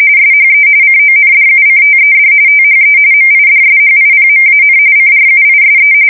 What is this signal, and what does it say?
RTTY, about 45.45 baud. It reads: SHE CONDUCTED THE MIDSHIPMAN CRUISE
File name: fsk170 / battleshipnj